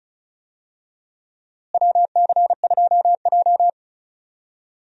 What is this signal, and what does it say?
Morse code, heard as a keyed tone, about 36 wpm, 700 Hz. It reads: WC2J
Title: USA Call Signs - 4 Characters - Repeated 3x - 35wpm